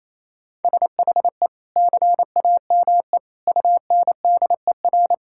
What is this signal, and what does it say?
Morse code, heard as a keyed tone, about 27 wpm, 700 Hz. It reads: SHE CAME UNDER
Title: Sentences from Top 1000 Words - Review Part 6 - 28wpm